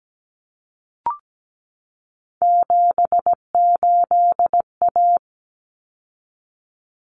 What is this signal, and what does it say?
Morse code, heard as a keyed tone, about 17 wpm, 700 Hz. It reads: 78A
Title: Common Cut Numbers - 3-Digit 17wpm